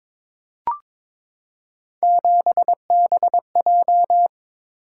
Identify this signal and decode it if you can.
Morse code, heard as a keyed tone, about 22 wpm, 700 Hz. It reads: 7BJ
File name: practice / 22wpm / Three Letter-Number Combinations - 22wpm